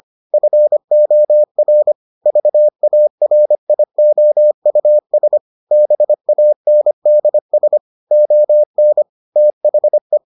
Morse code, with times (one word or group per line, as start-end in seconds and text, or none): 0.3-1.9 FOR
2.3-5.4 VARIOUS
5.7-7.8 BANDS
8.1-9.0 ON
9.4-10.2 THE